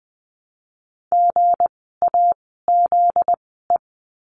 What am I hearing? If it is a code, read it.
Morse code, heard as a keyed tone, about 20 wpm, 700 Hz. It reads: G A Z E
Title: 244 Lesson 39f - Letter z - Words 20wpm